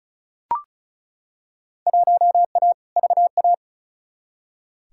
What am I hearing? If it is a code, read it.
Morse code, heard as a keyed tone, about 36 wpm, 700 Hz. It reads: 1A VA
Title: Field Day Exchanges - 35wpm